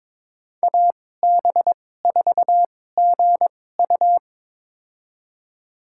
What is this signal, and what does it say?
Morse code, heard as a keyed tone, about 22 wpm, spaced as for 16 wpm, 700 Hz. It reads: AB4GU
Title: USA Call Signs - 5 Characters - 2x Character Spacing - 22wpm